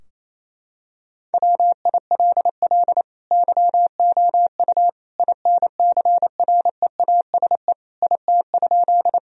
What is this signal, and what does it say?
Morse code, heard as a keyed tone, about 28 wpm, 700 Hz. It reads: WILL YOU INCREASE IT?
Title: Sentences from Top 1000 Words - Review Part 4 - 28wpm